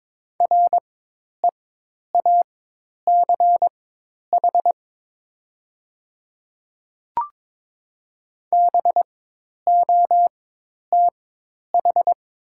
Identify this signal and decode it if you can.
Morse code, heard as a keyed tone, about 22 wpm, spaced as for 10 wpm, 700 Hz. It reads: REACH BOTH
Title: 4x Character Spacing - Top 500 Words - 22wpm